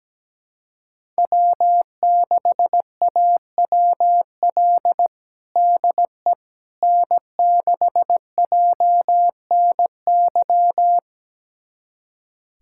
Morse code, heard as a keyed tone, about 17 wpm, 700 Hz. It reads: W6AWL DE N6JNY
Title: USA Callsign DE Callsign - 5 Characters - 17wpm